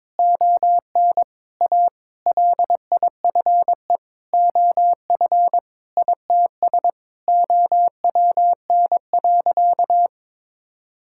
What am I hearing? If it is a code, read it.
Morse code, heard as a keyed tone, about 22 wpm, 700 Hz. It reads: ON A LIFE OF ITS OWN.
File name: Sentences with 7-Word Phrases - 22wpm